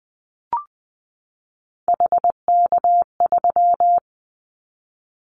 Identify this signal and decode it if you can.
Morse code, heard as a keyed tone, about 20 wpm, 700 Hz. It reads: HK3
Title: Callsign Prefixes for Top 100 DXCC - 20wpm